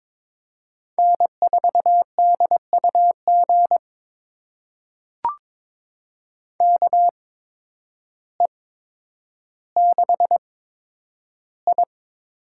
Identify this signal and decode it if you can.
Morse code, heard as a keyed tone, about 22 wpm, spaced as for 10 wpm, 700 Hz. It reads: N4DUG KE6I
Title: USA Call Signs - 5 Characters - 8x Character Spacing - 22wpm